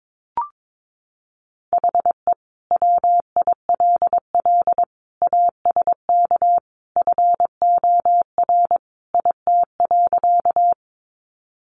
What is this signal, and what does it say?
Morse code, heard as a keyed tone, about 22 wpm, 700 Hz. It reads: HE WILL ASK FOR IT.
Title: Sentences from Top 200 Words 22wpm